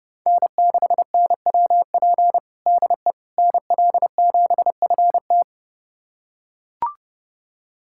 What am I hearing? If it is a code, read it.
Morse code, heard as a keyed tone, about 30 wpm, 700 Hz. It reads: N6NWP DE NL7FT
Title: USA Callsign DE Callsign - 5 Characters - 30wpm